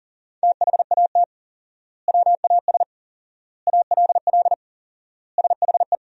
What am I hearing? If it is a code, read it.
Morse code, heard as a keyed tone, about 40 wpm, 700 Hz. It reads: THAT WAS ALL SHE
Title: 4x Word-Spacing - Sentences from Top 100 Words - 40wpm